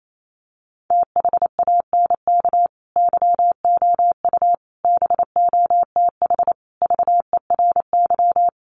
Morse code, heard as a keyed tone, about 28 wpm, 700 Hz. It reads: THANK YOU BOTH VERY
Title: Sets of 5 Words 28wpm